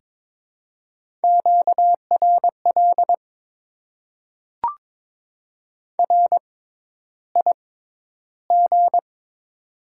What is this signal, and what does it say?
Morse code, heard as a keyed tone, about 21 wpm, spaced as for 11 wpm, 690 Hz. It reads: QRL RIG
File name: Top 20 QSO Elements - 6x Character Spacing - 22wpm